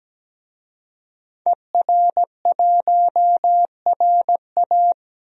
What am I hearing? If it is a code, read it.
Morse code, heard as a keyed tone, about 17 wpm, 700 Hz. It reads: ER1RA
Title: DXCC-100 Call Signs - 5 Characters - 17wpm